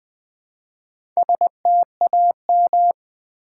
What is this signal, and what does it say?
Morse code, heard as a keyed tone, about 20 wpm, 700 Hz. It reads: STAM
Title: Build A Word - 1 Letter At A Time - 5-Letter Words - 20wpm